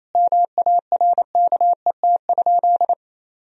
Morse code, heard as a keyed tone, about 27 wpm, 700 Hz. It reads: MARKET?
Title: Sentences from Top 1000 Words - Review Part 3 - 28wpm